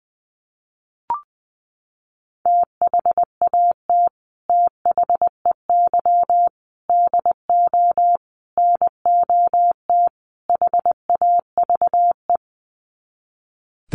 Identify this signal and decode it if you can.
Morse code, heard as a keyed tone, about 20 wpm, 700 Hz. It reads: THAT THEY DO NOT HAVE
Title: Sets of 5 Words Encore2 20wpm